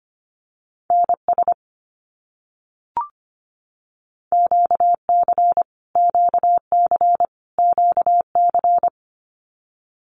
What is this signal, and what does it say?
Morse code, heard as a keyed tone, about 25 wpm, 700 Hz. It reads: NS QC QC QC
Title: Canadian Provinces and Territories - Repeated 3x - 25wpm